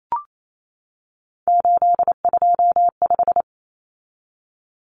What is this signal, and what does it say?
Morse code, heard as a keyed tone, about 28 wpm, 700 Hz. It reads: 825
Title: Common Cut Numbers - 3-Digit 28wpm